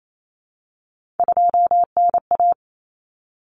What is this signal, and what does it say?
Morse code, heard as a keyed tone, about 28 wpm, 700 Hz. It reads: 2NA
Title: Common Cut Numbers - 3-Digit 28wpm